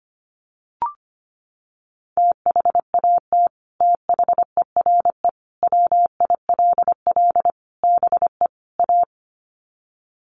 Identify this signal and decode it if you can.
Morse code, heard as a keyed tone, about 25 wpm, 700 Hz. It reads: THAT THERE WILL BE A
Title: Sets of 5 Words Encore2 25wpm